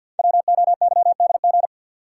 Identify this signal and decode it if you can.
Morse code, heard as a keyed tone, about 50 wpm, 690 Hz. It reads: WOYDG
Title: Code Groups - 5 Letters - 50wpm